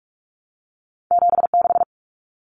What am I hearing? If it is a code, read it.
Morse code, heard as a keyed tone, about 44 wpm, 700 Hz. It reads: 76
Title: Numbers - 2-Digit 45wpm